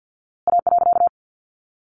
Morse code, heard as a keyed tone, about 50 wpm, 700 Hz. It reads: A.
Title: A Sound Of Thunder 50wpm